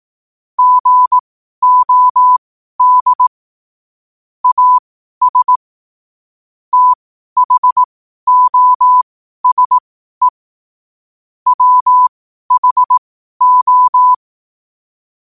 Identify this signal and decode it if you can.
Morse code, heard as a keyed tone, about 18 wpm, spaced as for 12 wpm, 1000 Hz. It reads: GOD AS THOSE WHO
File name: qotdfiles / 12wpm